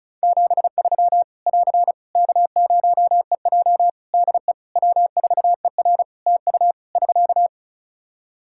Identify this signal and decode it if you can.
Morse code, heard as a keyed tone, about 36 wpm, 700 Hz. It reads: 73 <AR> K0EJ DE W4ER TU <SK>
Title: Practice QSO - Exchange 4 - Ending - 35wpm